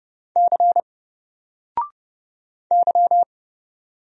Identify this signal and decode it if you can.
Morse code, heard as a keyed tone, about 30 wpm, 700 Hz. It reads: C Y
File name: Single-Letter-Number-30wpm